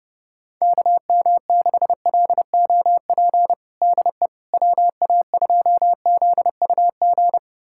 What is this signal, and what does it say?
Morse code, heard as a keyed tone, about 30 wpm, 700 Hz. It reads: KM6LOP DE WA2ZUG
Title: USA Callsign DE Callsign - 6 Characters - 30wpm